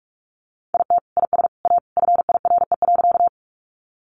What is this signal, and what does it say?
Morse code, heard as a keyed tone, about 46 wpm, 700 Hz. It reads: IT IS A FIRE.
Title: Sentences from Top 400 Words 45wpm